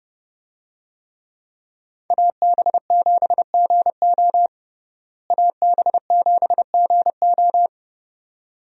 Morse code, heard as a keyed tone, about 32 wpm, 700 Hz. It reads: AB7GO AB7GO
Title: USA Call Signs - 5 Characters - Repeated 2x - 30wpm